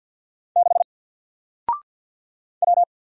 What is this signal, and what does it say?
Morse code, heard as a keyed tone, about 50 wpm, 700 Hz. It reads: 6 W
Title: Single Letter-Number Alt 50wpm